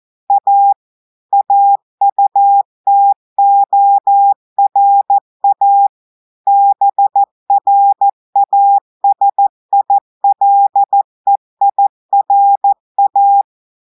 Morse code, heard as a keyed tone, about 14 wpm, 800 Hz. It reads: A AUTORA BRASILEIRA
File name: audios / portugais / portugais texte en clair 1 vitesse 14 mots minutes